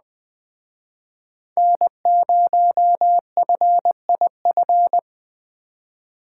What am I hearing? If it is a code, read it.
Morse code, heard as a keyed tone, about 20 wpm, 700 Hz. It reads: N0FIF
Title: Speed Racing - USA 1x3 Call Signs 17wpm